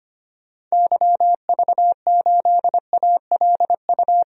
Morse code, heard as a keyed tone, about 25 wpm, 700 Hz. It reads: YV8ALU
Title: DXCC-100 Call Signs - 6 Characters - 25wpm